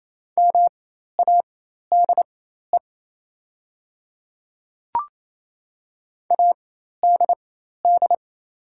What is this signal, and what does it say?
Morse code, heard as a keyed tone, about 28 wpm, spaced as for 13 wpm, 700 Hz. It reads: MADE ADD
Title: 4x Character Spacing - Top 500 Words - 28wpm